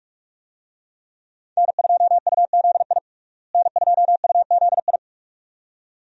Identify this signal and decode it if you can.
Morse code, heard as a keyed tone, about 41 wpm, 700 Hz. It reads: N2UZI N2UZI
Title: USA Call Signs - 5 Characters - Repeated 2x - 45wpm